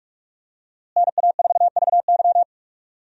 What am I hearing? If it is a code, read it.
Morse code, heard as a keyed tone, about 46 wpm, 700 Hz. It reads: NA4VY
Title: USA Call Signs - 5 Characters - Repeated 2x - 45wpm